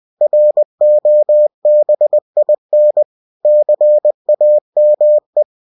Morse code, heard as a keyed tone, about 20 wpm, 600 Hz. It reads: ROBIN CAME
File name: cw-books / winnie-the-pooh / winnie-20wpm-page-0025